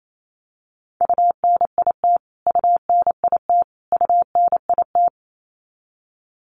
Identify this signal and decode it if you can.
Morse code, heard as a keyed tone, about 28 wpm, 700 Hz. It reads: UNIT UNIT UNIT
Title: Top 500 Words Repeated 28wpm